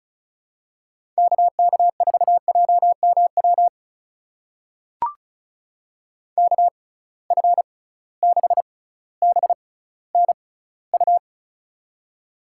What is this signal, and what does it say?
Morse code, heard as a keyed tone, about 34 wpm, spaced as for 18 wpm, 700 Hz. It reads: KK4JMW KF6BNU
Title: USA Call Signs - 6 Characters - 6x Character Spacing - 35wpm